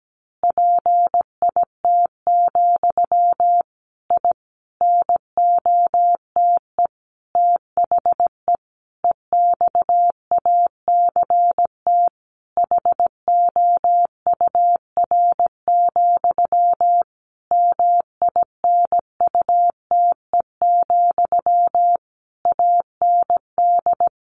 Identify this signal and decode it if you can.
Morse code, heard as a keyed tone, about 17 wpm, 700 Hz. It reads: PIT, I NOTE THE EXACT HOUR, MINUTE, AND
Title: A Sound Of Thunder 17wpm